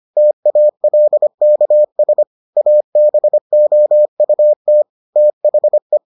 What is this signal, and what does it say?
Morse code, heard as a keyed tone, about 25 wpm, 600 Hz. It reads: TALKS ABOUT THE